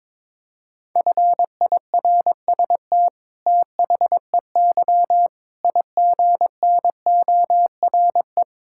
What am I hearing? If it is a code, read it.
Morse code, heard as a keyed tone, about 21 wpm, 700 Hz. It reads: FIRST THEY IGNORE
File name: practice / 22wpm / Sets of 8 Words - 22wpm